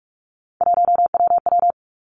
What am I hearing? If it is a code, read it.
Morse code, heard as a keyed tone, about 46 wpm, 700 Hz. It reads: 1WW
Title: Three Challenging Letter-Number Combinations 45wpm